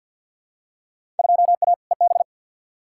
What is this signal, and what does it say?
Morse code, heard as a keyed tone, about 50 wpm, 700 Hz. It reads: 2A EB
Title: Field Day Exchanges - 50wpm